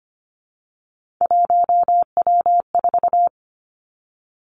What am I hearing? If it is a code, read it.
Morse code, heard as a keyed tone, about 25 wpm, 700 Hz. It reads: 1W4
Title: Three Challenging Letter-Number Combinations 25wpm